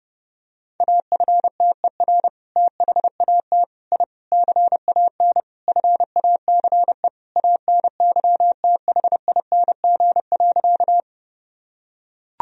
Morse code, heard as a keyed tone, about 29 wpm, 700 Hz. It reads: AFTER THAT I CAN FACE ANYTHING.
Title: Sentences from Top 1000 Words - Review Part 2 - 30wpm